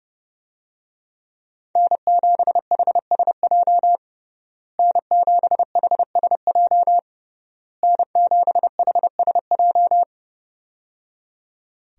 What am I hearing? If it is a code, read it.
Morse code, heard as a keyed tone, about 30 wpm, 700 Hz. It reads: N7HSJ N7HSJ N7HSJ
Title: USA Call Signs - 5 Characters - Repeated 3x - 30wpm